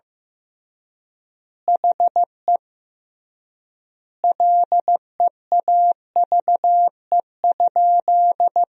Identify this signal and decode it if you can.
Morse code, heard as a keyed tone, about 15 wpm, 700 Hz. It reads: HE LEAVE?
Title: 3x Word-Spacing - Sentences from Top 400 Words - 15wpm